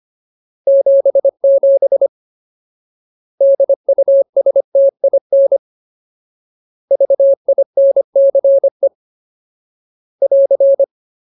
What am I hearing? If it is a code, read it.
Morse code, heard as a keyed tone, about 25 wpm, 550 Hz. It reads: 77 DUSTIN VINCE <AR>
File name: Ultimate Word List / 25WPM Ultimate